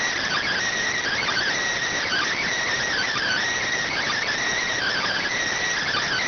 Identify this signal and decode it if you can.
Emergency Alert System (SAME) tones (header bursts), heard as harsh buzzing bursts. It is heard from 1.0 to 5.4 s.